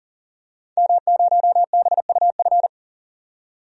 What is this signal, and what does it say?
Morse code, heard as a keyed tone, about 40 wpm, 700 Hz. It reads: M0BUF
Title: DXCC-100 Call Signs - 5 Characters - 40wpm